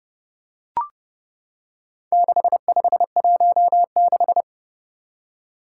Morse code, heard as a keyed tone, about 30 wpm, 700 Hz. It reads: 6516
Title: Four Challenging Letter-Number Combinations 30wpm